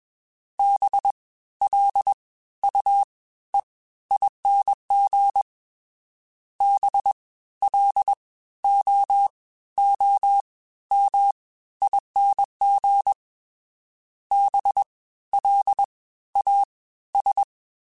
Morse code, heard as a keyed tone, about 21 wpm, 800 Hz. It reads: B L U E ING B L O O M ING B L A S
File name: Special INGs-mp3s / Special INGs-mp3s / Fast ing @20